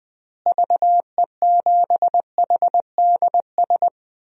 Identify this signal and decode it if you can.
Morse code, heard as a keyed tone, about 20 wpm, 700 Hz. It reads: VE7HDS
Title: DXCC-100 Call Signs - 6 Characters - Repeated 2x - 20wpm